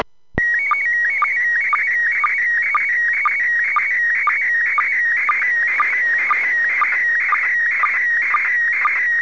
SSTV (slow-scan television), PD 120.
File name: SSTV-Signal-31.02.19-17.35Uhr-JN39vk-via-DL0HSK-A